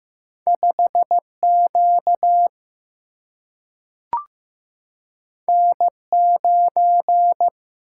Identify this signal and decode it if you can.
Morse code, heard as a keyed tone, about 15 wpm, 700 Hz. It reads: 5 Q N 9
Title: Single-Letter Single-Number 15wpm